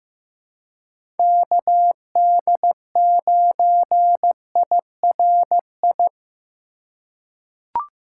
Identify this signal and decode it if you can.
Morse code, heard as a keyed tone, about 15 wpm, 700 Hz. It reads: KD9IRI
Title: USA Call Signs - 6 Characters - 2x Character Spacing - 15wpm